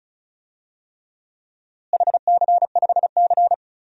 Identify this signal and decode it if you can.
Morse code, heard as a keyed tone, about 34 wpm, 700 Hz. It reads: HC5C
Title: DXCC-100 Call Signs - 4 Characters - Repeated 3x - 35wpm